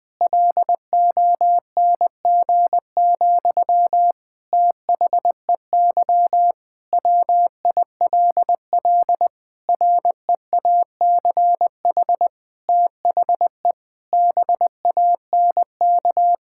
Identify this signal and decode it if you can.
Morse code, heard as a keyed tone, about 20 wpm, 700 Hz. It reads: LONG, THEY WILL REACH THE BANK
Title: Sentences from Top 1000 Words - Review Part 6 - 20wpm